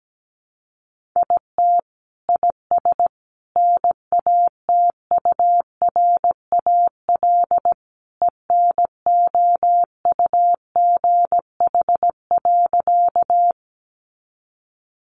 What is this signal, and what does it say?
Morse code, heard as a keyed tone, about 17 wpm, 700 Hz. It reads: IT IS NATURAL ENOUGH.
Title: Sentences from Top 700 Words 17wpm